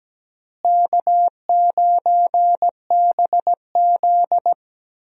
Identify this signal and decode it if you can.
Morse code, heard as a keyed tone, about 17 wpm, 700 Hz. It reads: K9BZ
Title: USA Call Signs - 4 Characters - Repeated 3x - 17wpm